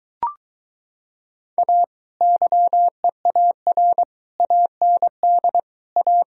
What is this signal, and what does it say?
Morse code, heard as a keyed tone, about 23 wpm, 700 Hz. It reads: A YEAR AND A